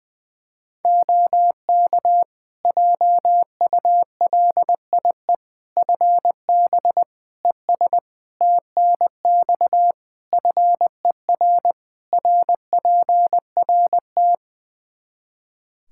Morse code, heard as a keyed tone, about 21 wpm, 700 Hz. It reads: OK JULIE FB ES TNX FER RPRT
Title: Practice QSO - Exchange 2 - Info - 20wpm